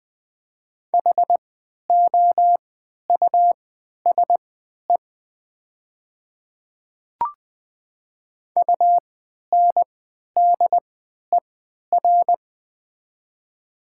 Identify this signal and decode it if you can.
Morse code, heard as a keyed tone, about 21 wpm, spaced as for 11 wpm, 700 Hz. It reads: HOUSE UNDER
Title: 3x Character Spacing - Top 200 Words - 20wpm